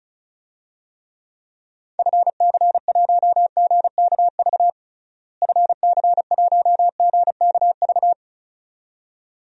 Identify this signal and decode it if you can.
Morse code, heard as a keyed tone, about 36 wpm, 700 Hz. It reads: FC1GKV FC1GKV
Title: DXCC-100 Call Signs - 6 Characters - Repeated 2x - 35wpm